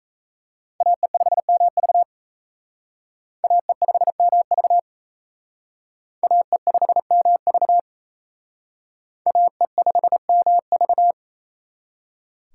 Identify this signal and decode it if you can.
Morse code, heard as a keyed tone, about 36 wpm, 700 Hz. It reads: AE5MV AE5MV AE5MV AE5MV